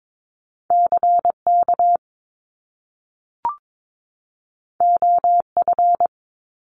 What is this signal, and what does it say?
Morse code, heard as a keyed tone, about 22 wpm, 700 Hz. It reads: CK OF
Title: Top Two Letter Combinations in Words 22wpm